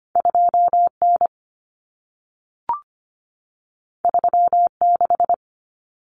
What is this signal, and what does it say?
Morse code, heard as a keyed tone, about 25 wpm, 700 Hz. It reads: 2N 36
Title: Common Cut Numbers - 2-Digit 25wpm